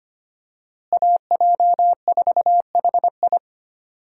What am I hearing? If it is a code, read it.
Morse code, heard as a keyed tone, about 25 wpm, 700 Hz. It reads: AJ4HI
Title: USA Call Signs - 5 Characters - Repeated 3x - 25wpm